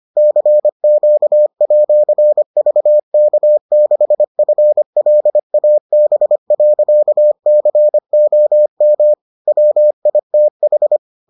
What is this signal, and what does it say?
Morse code, heard as a keyed tone, about 25 wpm, 600 Hz. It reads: CQ@VK6FLAB.COM WITH